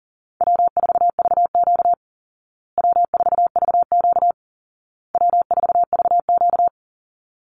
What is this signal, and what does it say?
Morse code, heard as a keyed tone, about 40 wpm, 700 Hz. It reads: W4VQ W4VQ W4VQ
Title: Easy Call Signs Repeated 3x - 40wpm